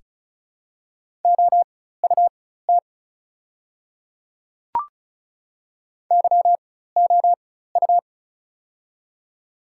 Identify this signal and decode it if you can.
Morse code, heard as a keyed tone, about 36 wpm, spaced as for 16 wpm, 700 Hz. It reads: OUT YOU
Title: 4x Character Spacing - Top 100 Words - 35wpm